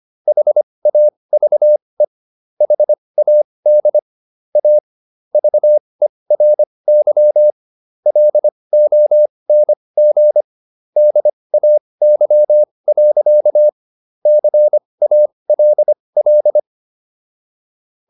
Morse code, heard as a keyed tone, about 25 wpm, spaced as for 20 wpm, 600 Hz. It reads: HAVE HAD A VERY LONG DAY. CALL